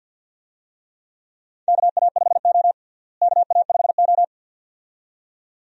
Morse code, heard as a keyed tone, about 50 wpm, 700 Hz. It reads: KA5O KA5O
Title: USA Call Signs - 4 Characters - Repeated 2x - 50wpm